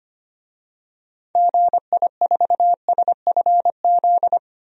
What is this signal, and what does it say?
Morse code, heard as a keyed tone, about 25 wpm, 700 Hz. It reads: GI4SFZ
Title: DXCC-100 Call Signs - 6 Characters - Repeated 2x - 25wpm